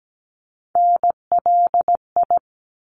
Morse code, heard as a keyed tone, about 17 wpm, 700 Hz. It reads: NLI
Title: ARRL Sections 17wpm